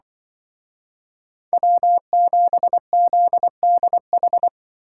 Morse code, heard as a keyed tone, about 23 wpm, 700 Hz. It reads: W7ZDH
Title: Speed Racing - USA Call Signs 20wpm